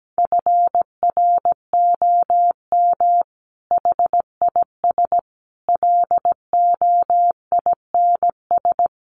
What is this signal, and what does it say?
Morse code, heard as a keyed tone, about 17 wpm, 700 Hz. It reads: FROM HIS LOINS
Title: A Sound Of Thunder 17wpm